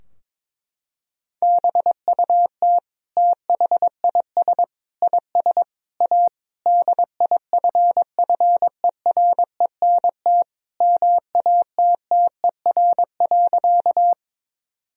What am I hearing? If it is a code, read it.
Morse code, heard as a keyed tone, about 22 wpm, 700 Hz. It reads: BUT THIS IS A DIFFERENT MATTER.
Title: Sentences from Top 1000 Words - Review Part 3 - 22wpm